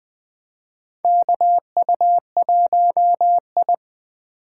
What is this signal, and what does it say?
Morse code, heard as a keyed tone, about 20 wpm, 700 Hz. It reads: KU1I
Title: USA Call Signs - 4 Characters - 2x Character Spacing - 20wpm